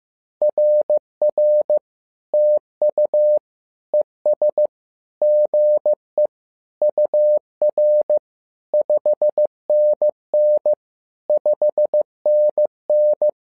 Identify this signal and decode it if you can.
Morse code, heard as a keyed tone, about 15 wpm, 600 Hz. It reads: RR TU ES GE UR 5NN 5NN
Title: SOTA QSO Practice 15wpm